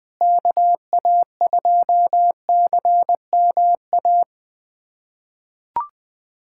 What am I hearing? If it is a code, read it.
Morse code, heard as a keyed tone, about 20 wpm, 700 Hz. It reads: KA2CMA
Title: USA Callsign DE Callsign - 6 Characters - 20wpm